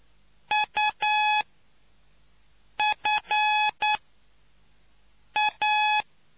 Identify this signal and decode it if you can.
Morse code, heard as a keyed tone, about 9 wpm, spaced as for 5 wpm, 850 Hz. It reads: UFA